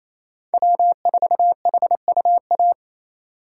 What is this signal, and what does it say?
Morse code, heard as a keyed tone, about 28 wpm, 700 Hz. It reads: W4HUA
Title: USA Call Signs - 5 Characters - 4x Character Spacing - 28wpm